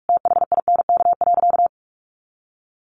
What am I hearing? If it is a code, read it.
Morse code, heard as a keyed tone, about 44 wpm, 700 Hz. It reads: THINK.
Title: Sentences from Top 400 Words 45wpm